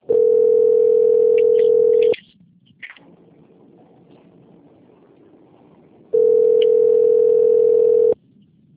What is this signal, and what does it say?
Ringback tone (North America), 440 + 480 Hz.